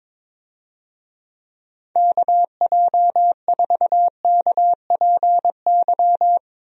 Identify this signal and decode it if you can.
Morse code, heard as a keyed tone, about 22 wpm, 700 Hz. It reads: KJ4KPY
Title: USA Call Signs - 6 Characters - Repeated 2x - 22wpm